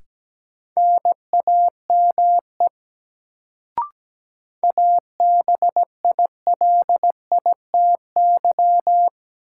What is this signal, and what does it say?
Morse code, heard as a keyed tone, about 17 wpm, 700 Hz. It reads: NAME ABILITY
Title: Sets of 3 Words - New Word Drill - 17wpm